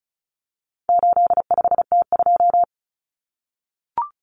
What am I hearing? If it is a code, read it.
Morse code, heard as a keyed tone, about 34 wpm, 700 Hz. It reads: 85T2
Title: Common Cut Numbers - 4-Digit 35wpm